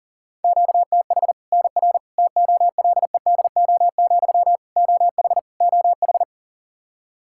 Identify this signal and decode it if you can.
Morse code, heard as a keyed tone, about 40 wpm, 700 Hz. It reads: QTH NR TOLEDO, OH OH
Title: Practice QSO - Exchange 1 - Beginning - 40wpm